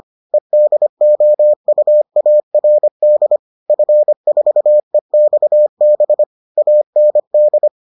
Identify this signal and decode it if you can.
Morse code, heard as a keyed tone, about 25 wpm, 600 Hz. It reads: EDOUARD F4EXB AND